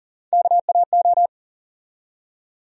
Morse code, heard as a keyed tone, about 39 wpm, 700 Hz. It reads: KAO
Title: USA Call Sign Postfixes - 3 Letters 40wpm